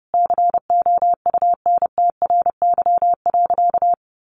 Morse code, heard as a keyed tone, about 30 wpm, 700 Hz. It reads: COUNTRY.
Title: Sentences from Top 300 Words 30wpm